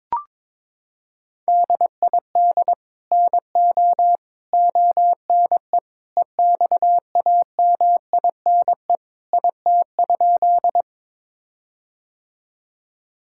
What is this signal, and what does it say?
Morse code, heard as a keyed tone, about 22 wpm, 700 Hz. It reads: DID NO ONE EXAMINE IT?
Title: Sentences from Top 1600 Words 22wpm